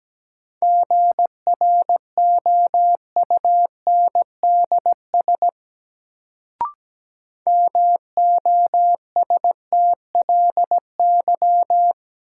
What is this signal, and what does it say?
Morse code, heard as a keyed tone, about 17 wpm, 700 Hz. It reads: GROUNDS MOSTLY
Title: Sentences from Top 1300 Words - New Word Drill - 17wpm